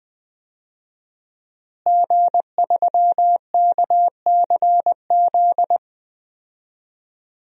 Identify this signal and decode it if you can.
Morse code, heard as a keyed tone, about 20 wpm, 700 Hz. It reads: G3KCZ
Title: DXCC-100 Call Signs - 5 Characters - Repeated 2x - 20wpm